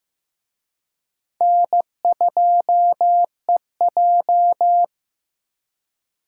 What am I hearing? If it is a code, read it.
Morse code, heard as a keyed tone, about 18 wpm, 700 Hz. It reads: N2EJ
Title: Speed Racing - USA 1x2 Call Signs 15wpm